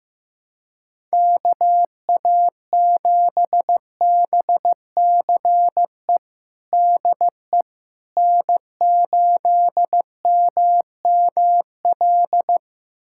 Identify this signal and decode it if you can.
Morse code, heard as a keyed tone, about 15 wpm, 700 Hz. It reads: KA7BCE DE N8MML
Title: USA Callsign DE Callsign 15wpm